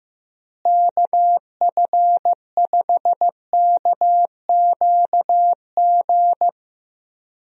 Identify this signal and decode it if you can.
Morse code, heard as a keyed tone, about 15 wpm, 700 Hz. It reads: KF5KQG
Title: USA Call Signs - 6 Characters - Repeated 2x - 15wpm